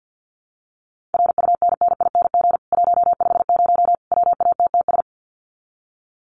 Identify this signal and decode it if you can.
Morse code, heard as a keyed tone, about 50 wpm, 700 Hz. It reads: RUNNING 150 WATTS
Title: Common QSO Phrases 50wpm